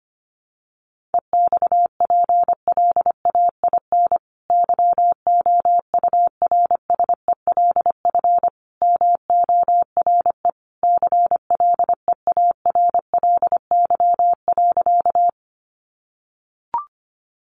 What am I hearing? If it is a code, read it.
Morse code, heard as a keyed tone, about 25 wpm, 700 Hz. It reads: EXPLAIN YOURSELF MORE CLEARLY.
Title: Sentences from Top 1300 Words 25wpm